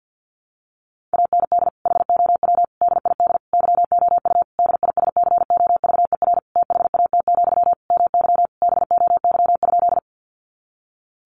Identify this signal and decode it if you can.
Morse code, heard as a keyed tone, about 48 wpm, 700 Hz. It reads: AND HOW DID YOU DISCOVER THAT, MY BOY?
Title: Sentences from Top 1100 Words 50wpm